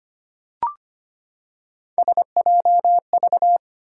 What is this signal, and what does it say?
Morse code, heard as a keyed tone, about 25 wpm, 700 Hz. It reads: SJV
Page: morse-code-ninja-practice-25wpm